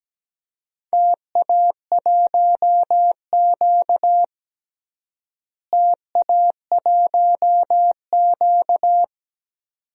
Morse code, heard as a keyed tone, about 17 wpm, 700 Hz. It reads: TA1Q TA1Q
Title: DXCC-100 Call Signs - 4 Characters - Repeated 2x - 17wpm